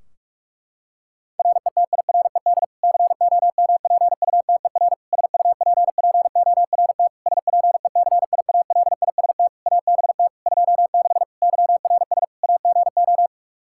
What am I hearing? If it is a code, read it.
Morse code, heard as a keyed tone, about 46 wpm, 700 Hz. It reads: RETIRED COMPUTER SUPPORT SPECIALIST ABT 26 YRS AGO
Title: Practice QSO - Exchange 3 - More Info - 45wpm